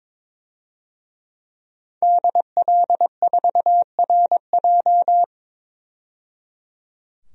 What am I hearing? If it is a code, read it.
Morse code, heard as a keyed tone, about 22 wpm, 700 Hz. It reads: DL4RJ
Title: DXCC-100 Call Signs - 5 Characters - 22wpm